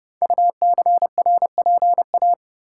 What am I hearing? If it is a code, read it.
Morse code, heard as a keyed tone, about 28 wpm, 700 Hz. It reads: UCRPA
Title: Code Groups - 5 Letter:Numbers - 30wpm